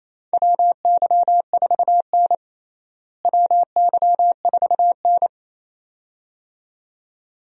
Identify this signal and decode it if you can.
Morse code, heard as a keyed tone, about 27 wpm, 700 Hz. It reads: WY4N WY4N
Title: USA Call Signs - 4 Characters - Repeated 2x - 28wpm